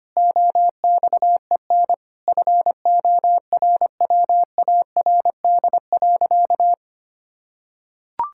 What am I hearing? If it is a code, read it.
Morse code, heard as a keyed tone, about 25 wpm, 700 Hz. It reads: OXEN FORWARD.
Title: Pangram Challenge v2 - 25wpm